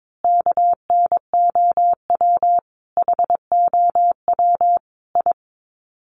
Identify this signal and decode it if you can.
Morse code, heard as a keyed tone, about 22 wpm, 700 Hz. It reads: KNOW HOW I
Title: Sets of 5 Words Encore2 22wpm